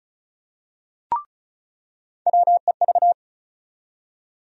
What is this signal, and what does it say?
Morse code, heard as a keyed tone, about 34 wpm, 700 Hz. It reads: WEV
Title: USA Call Sign Postfixes - 3 Letters 35wpm